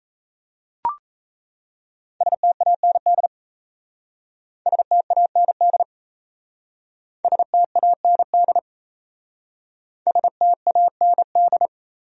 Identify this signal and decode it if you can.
Morse code, heard as a keyed tone, about 35 wpm, 700 Hz. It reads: STAND STAND STAND STAND